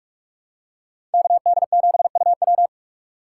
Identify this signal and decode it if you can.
Morse code, heard as a keyed tone, about 44 wpm, 700 Hz. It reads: KD7UW
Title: USA Call Signs - 5 Characters - 6x Character Spacing - 45wpm